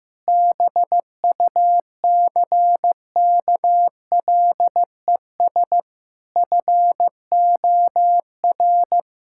Morse code, heard as a keyed tone, about 16 wpm, 700 Hz. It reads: BUCKLES FOR
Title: Pangram Challenge v2 - 15wpm